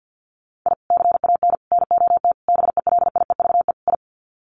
Morse code, heard as a keyed tone, about 50 wpm, 700 Hz. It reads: I CAN NOT BELIEVE I
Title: Sets of 5 Words Encore2 50wpm